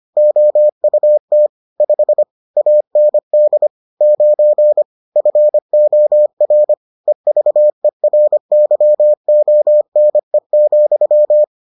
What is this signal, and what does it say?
Morse code, heard as a keyed tone, about 25 wpm, 600 Hz. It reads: OUT 5 AND 9 FOR EVERYONE,